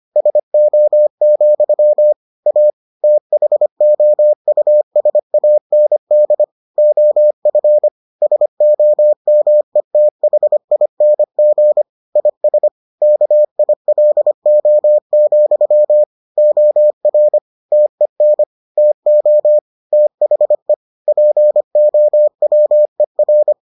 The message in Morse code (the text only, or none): SO, A THOUSAND OF SOMETHING IS KILO, OR TEN TO THE POWER